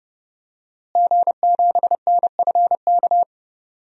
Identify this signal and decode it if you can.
Morse code, heard as a keyed tone, about 30 wpm, 700 Hz. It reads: G7NFK
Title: DXCC-100 Call Signs - 5 Characters - Repeated 3x - 30wpm